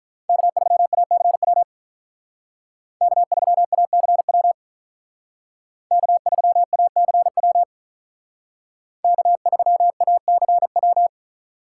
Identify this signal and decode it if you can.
Morse code, heard as a keyed tone, about 42 wpm, 700 Hz. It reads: K3ACW K3ACW K3ACW K3ACW
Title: Speed Racing - USA 1x3 Call Signs 35wpm